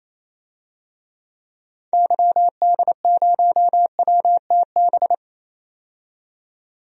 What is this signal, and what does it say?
Morse code, heard as a keyed tone, about 28 wpm, 700 Hz. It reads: YD0WTB
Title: DXCC-100 Call Signs - 6 Characters - 28wpm